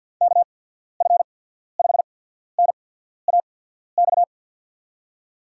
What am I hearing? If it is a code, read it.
Morse code, heard as a keyed tone, about 51 wpm, spaced as for 14 wpm, 700 Hz. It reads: KF5NAX
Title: USA Call Signs - 6 Characters - 8x Character Spacing - 50wpm